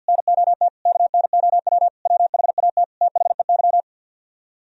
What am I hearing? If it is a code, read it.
Morse code, heard as a keyed tone, about 62 wpm, 700 Hz. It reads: NOT KNOW WHAT THEY
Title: Speed-Racing - Sets of 5 Words 50wpm